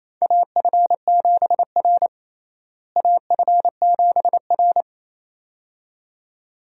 Morse code, heard as a keyed tone, about 28 wpm, 700 Hz. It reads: AF7R AF7R
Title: USA Call Signs - 4 Characters - Repeated 2x - 28wpm